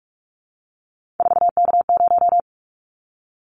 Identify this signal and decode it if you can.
Morse code, heard as a keyed tone, about 46 wpm, 700 Hz. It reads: 4K0
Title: Callsign Prefixes for Top 100 DXCC - 45wpm